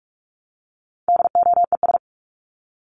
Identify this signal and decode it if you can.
Morse code, heard as a keyed tone, about 44 wpm, 700 Hz. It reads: DOES
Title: ES:S Suffix - Top 100 Words with Base Words 45wpm